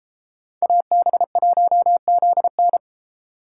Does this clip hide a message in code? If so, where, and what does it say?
Yes: Morse code from 0.6 to 2.8 s, heard as a keyed tone, about 29 wpm, 700 Hz. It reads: AB1ZN